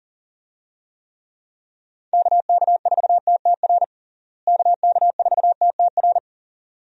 Morse code, heard as a keyed tone, about 40 wpm, 700 Hz. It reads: KK4TTR KK4TTR
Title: USA Call Signs - 6 Characters - Repeated 2x - 40wpm